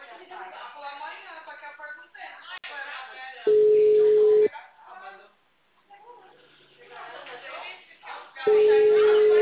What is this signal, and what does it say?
Ringback tone (Europe, 425 Hz), with speech.